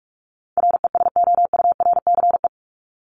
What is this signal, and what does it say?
Morse code, heard as a keyed tone, about 44 wpm, 700 Hz. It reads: RESOURCE
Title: ES:S Suffix - Top 100 Words with Base Words 45wpm